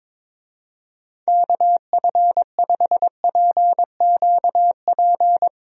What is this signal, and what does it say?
Morse code, heard as a keyed tone, about 22 wpm, 700 Hz. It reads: KF5PQP
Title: USA Call Signs - 6 Characters - Repeated 2x - 22wpm